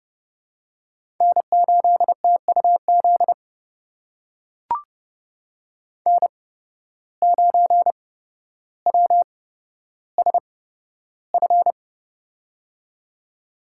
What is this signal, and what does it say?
Morse code, heard as a keyed tone, about 29 wpm, spaced as for 13 wpm, 700 Hz. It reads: N8TUZ N9WSF
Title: USA Call Signs - 5 Characters - 8x Character Spacing - 30wpm